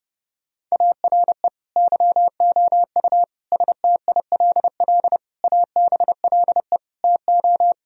Morse code, heard as a keyed tone, about 30 wpm, 700 Hz. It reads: ARE YOU STILL ABLE TO
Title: Sentences from Top 1000 Words - Review Part 2 - 30wpm